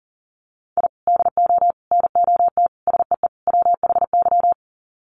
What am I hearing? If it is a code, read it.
Morse code, heard as a keyed tone, about 40 wpm, 700 Hz. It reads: I DO NOT SEE WHY
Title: Sets of 5 Words Encore 40wpm